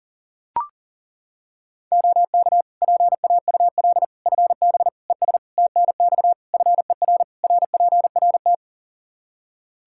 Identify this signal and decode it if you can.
Morse code, heard as a keyed tone, about 40 wpm, 700 Hz. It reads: OK PAUL FB ES TNX FER RPRT
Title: Practice QSO - Exchange 2 - Info - 40wpm